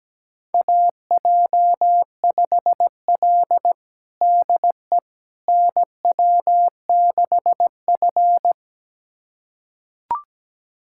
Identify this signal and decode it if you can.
Morse code, heard as a keyed tone, about 17 wpm, 700 Hz. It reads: AJ5L DE NW6F
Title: USA Callsign DE Callsign - 4 Characters - 17wpm